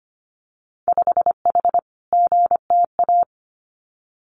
Morse code, heard as a keyed tone, about 25 wpm, 700 Hz. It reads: 5H GTA
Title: Winter Field Day Exchanges Encore 25wpm